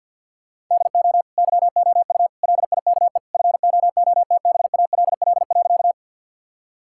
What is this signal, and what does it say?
Morse code, heard as a keyed tone, about 50 wpm, 700 Hz. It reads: DO YOU LIKE FOOTBALL.
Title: Sentences from Top 1600 Words 50wpm